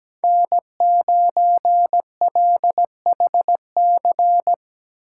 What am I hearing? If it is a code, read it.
Morse code, heard as a keyed tone, about 17 wpm, 700 Hz. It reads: N9LHC
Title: USA Call Signs - 3x Character Spacing - 17wpm